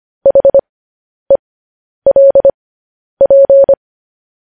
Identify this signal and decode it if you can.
Morse code, heard as a keyed tone, about 25 wpm, spaced as for 10 wpm, 550 Hz. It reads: HELP